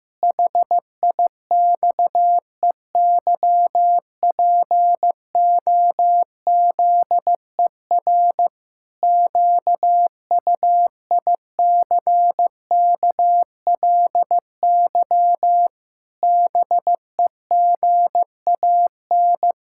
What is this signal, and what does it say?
Morse code, heard as a keyed tone, about 15 wpm, 700 Hz. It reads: HIXEYPOZER QUICKLY BEGAN
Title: Pangram Challenge v2 - 15wpm